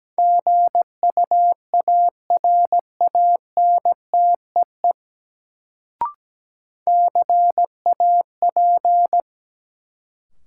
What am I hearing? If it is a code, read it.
Morse code, heard as a keyed tone, about 17 wpm, 700 Hz. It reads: GUARANTEE CAP
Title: Sentences from Top 1600 Words - New Word Drill Alt - 17wpm